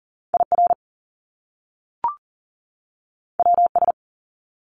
Morse code, heard as a keyed tone, about 40 wpm, 700 Hz. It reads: IR WS
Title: Top Two Letter Combinations in Words 40wpm